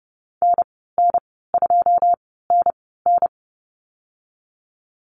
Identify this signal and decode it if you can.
Morse code, heard as a keyed tone, about 30 wpm, spaced as for 17 wpm, 700 Hz. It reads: NN2NN
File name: Lesson 14g - Number 2 - Call Signs 30wpm